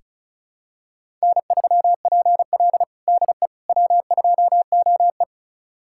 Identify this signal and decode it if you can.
Morse code, heard as a keyed tone, about 34 wpm, 700 Hz. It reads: N3PL DE W2OE
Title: USA Callsign DE Callsign - 4 Characters - 35wpm